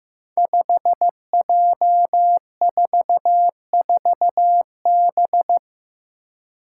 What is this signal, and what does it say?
Morse code, heard as a keyed tone, about 15 wpm, 700 Hz. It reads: 5J44B
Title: Five Challenging Letter-Number Combinations 15wpm